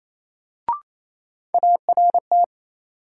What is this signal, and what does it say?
Morse code, heard as a keyed tone, about 28 wpm, 700 Hz. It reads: ART
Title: Sentences from Top 600 Words - New Word Drill - 28wpm